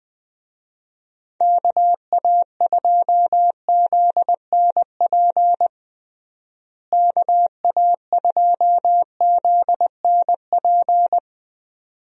Morse code, heard as a keyed tone, about 21 wpm, 700 Hz. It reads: KA2ZNP KA2ZNP
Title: USA Call Signs - 6 Characters - Repeated 2x - 20wpm